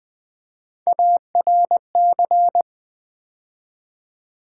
Morse code, heard as a keyed tone, about 20 wpm, 700 Hz. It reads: ARC
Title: Build A Word - 3 Letters At A Time - 9-Letter Words - 20wpm